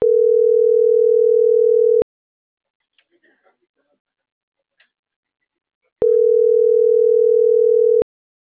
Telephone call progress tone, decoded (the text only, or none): ringback tone